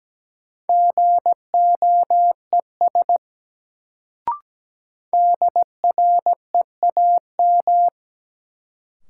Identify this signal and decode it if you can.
Morse code, heard as a keyed tone, about 17 wpm, 700 Hz. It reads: GOES DREAM
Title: Sentences from Top 1500 Words - New Word Drill Alt - 17wpm